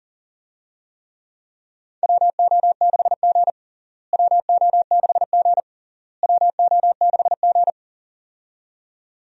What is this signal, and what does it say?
Morse code, heard as a keyed tone, about 40 wpm, 700 Hz. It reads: WO6G WO6G WO6G
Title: USA Call Signs - 4 Characters - Repeated 3x - 40wpm